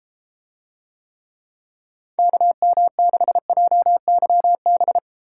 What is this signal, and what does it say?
Morse code, heard as a keyed tone, about 32 wpm, 700 Hz. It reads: KM6JYB